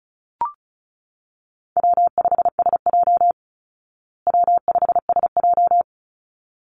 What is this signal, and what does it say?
Morse code, heard as a keyed tone, about 34 wpm, 700 Hz. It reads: W5SJ W5SJ
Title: Easy Call Signs Repeated 2x - 35wpm